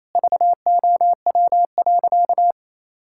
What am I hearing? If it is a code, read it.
Morse code, heard as a keyed tone, about 28 wpm, 700 Hz. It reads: VOW.
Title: Pangram Challenge v2 - 28wpm